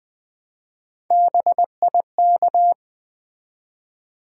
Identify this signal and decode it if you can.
Morse code, heard as a keyed tone, about 20 wpm, 700 Hz. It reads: BIK
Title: Build A Word - 1 Letter At A Time - 3 to 5-Letter Words - 20wpm